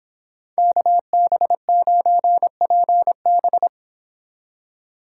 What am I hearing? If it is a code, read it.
Morse code, heard as a keyed tone, about 30 wpm, 700 Hz. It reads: KB9PB